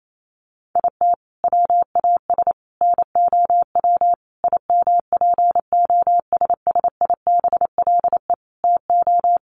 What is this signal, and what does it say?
Morse code, heard as a keyed tone, about 28 wpm, 700 Hz. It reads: IT WAS NOW IMPOSSIBLE TO
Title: Sentences from Top 1100 Words 28wpm